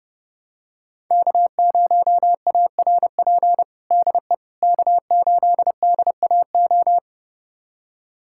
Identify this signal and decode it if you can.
Morse code, heard as a keyed tone, about 30 wpm, 700 Hz. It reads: K0ARP DE K8DAO
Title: USA Callsign DE Callsign - 5 Characters - 30wpm